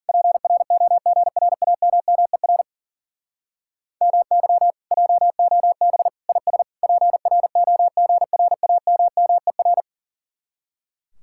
Morse code, heard as a keyed tone, about 50 wpm, 700 Hz. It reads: PROGRAMMER MY JOB IS PROGRAMMER
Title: Speed-Racing Common QSO Phrases - 40wpm